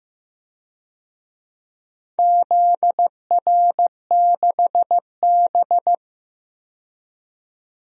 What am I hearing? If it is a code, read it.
Morse code, heard as a keyed tone, about 15 wpm, 700 Hz. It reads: ZR6B
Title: DXCC-100 Call Signs - 4 Characters - 15wpm